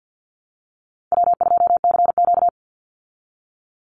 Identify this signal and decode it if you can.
Morse code, heard as a keyed tone, about 48 wpm, 700 Hz. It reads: W2CQ
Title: Call Signs Repeated 3x - 50wpm